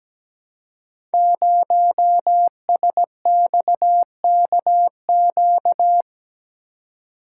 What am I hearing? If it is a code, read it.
Morse code, heard as a keyed tone, about 17 wpm, 700 Hz. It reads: 0SXKQ
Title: Code Groups - 5 Letter:Numbers - 17wpm